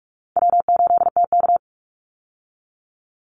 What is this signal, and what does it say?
Morse code, heard as a keyed tone, about 44 wpm, 700 Hz. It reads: W8TK
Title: Easy Call Signs - 45wpm